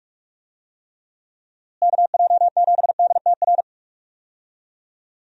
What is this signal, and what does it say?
Morse code, heard as a keyed tone, about 44 wpm, 700 Hz. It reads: KJ7DTR
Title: USA Call Signs - 6 Characters - Repeated 3x - 45wpm